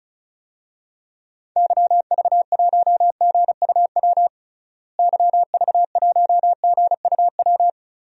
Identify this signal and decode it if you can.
Morse code, heard as a keyed tone, about 36 wpm, 700 Hz. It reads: YV1GUW YV1GUW
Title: DXCC-100 Call Signs - 6 Characters - Repeated 3x - 35wpm